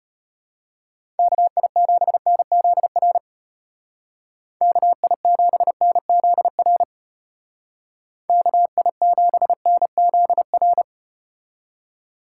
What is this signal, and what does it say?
Morse code, heard as a keyed tone, about 34 wpm, 700 Hz. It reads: KI7NZR KI7NZR KI7NZR